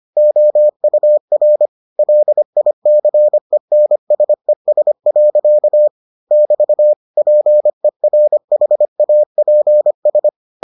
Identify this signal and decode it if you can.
Morse code, heard as a keyed tone, about 25 wpm, 600 Hz. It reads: OUR LICENSES. <BT> PERHAPS